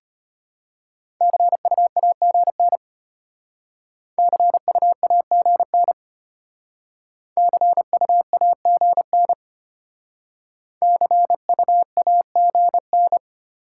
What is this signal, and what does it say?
Morse code, heard as a keyed tone, about 32 wpm, 690 Hz. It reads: CUAGN CUAGN CUAGN CUAGN